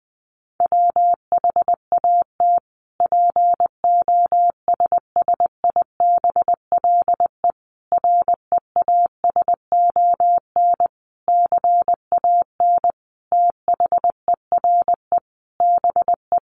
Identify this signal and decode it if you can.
Morse code, heard as a keyed tone, about 20 wpm, 700 Hz. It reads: WHAT POSSIBLE REASON CAN THERE BE
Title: Sentences from Top 1300 Words 20wpm